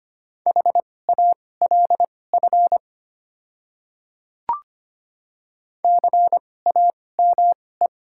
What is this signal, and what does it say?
Morse code, heard as a keyed tone, about 25 wpm, spaced as for 18 wpm, 700 Hz. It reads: HALF CAME
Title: 2x Character Spacing - Top 500 Words - 25wpm